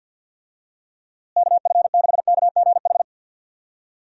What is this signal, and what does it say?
Morse code, heard as a keyed tone, about 52 wpm, 700 Hz. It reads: KF6KGH
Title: USA Call Signs - 6 Characters - Repeated 3x - 50wpm